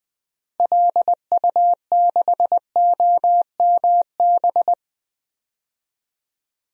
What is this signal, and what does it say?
Morse code, heard as a keyed tone, about 19 wpm, 700 Hz. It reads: LU6OMB
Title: DXCC-100 Call Signs - 6 Characters - Repeated 3x - 20wpm